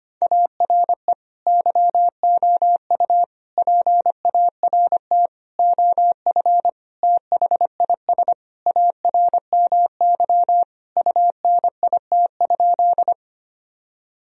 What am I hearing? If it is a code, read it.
Morse code, heard as a keyed tone, about 25 wpm, 700 Hz. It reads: ARE YOU PART OF THIS ARMY UNIT?
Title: Sentences from Top 1000 Words - Review Part 2 - 25wpm